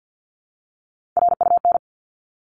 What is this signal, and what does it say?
Morse code, heard as a keyed tone, about 48 wpm, 700 Hz. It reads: RUN
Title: ING Suffix - Top 100 Words with Base Words 50wpm